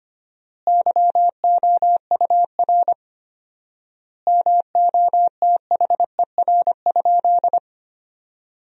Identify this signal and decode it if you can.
Morse code, heard as a keyed tone, about 25 wpm, 700 Hz. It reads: YOUR MOTHER?
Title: 4x Word-Spacing - Sentences from Top 200 Words - 25wpm